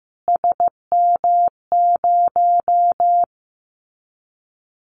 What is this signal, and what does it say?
Morse code, heard as a keyed tone, about 15 wpm, 700 Hz. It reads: SM0
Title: Callsign Prefixes for Top 25 DXCC - Repeated 3x - 15wpm